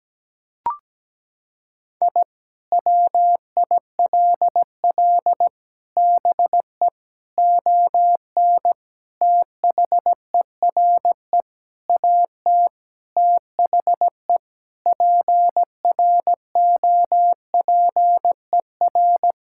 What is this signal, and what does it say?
Morse code, heard as a keyed tone, about 17 wpm, 700 Hz. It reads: I WILL BE ON THERE AT THE PROPER
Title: Sentences from Top 1000 Words - Review Part 5 - 17wpm